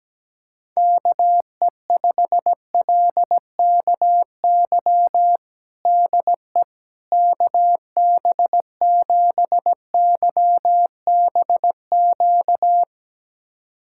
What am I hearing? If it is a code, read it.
Morse code, heard as a keyed tone, about 17 wpm, 700 Hz. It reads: KE5LKY DE KB7YBQ
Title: USA Callsign DE Callsign - 6 Characters - 17wpm